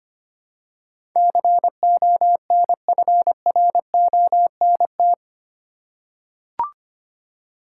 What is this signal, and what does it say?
Morse code, heard as a keyed tone, about 25 wpm, 700 Hz. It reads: CONFRONT
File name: Eight Letter Words - 25wpm